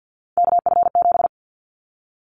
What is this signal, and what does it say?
Morse code, heard as a keyed tone, about 50 wpm, 700 Hz. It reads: KF7
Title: Callsign Prefixes for Top 25 DXCC - Repeated 3x - 50wpm